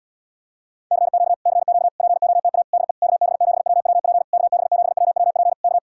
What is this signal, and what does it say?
Morse code, heard as a keyed tone, about 80 wpm, 700 Hz. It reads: CQ CQ POTA DE KB7WWP KB7WWP K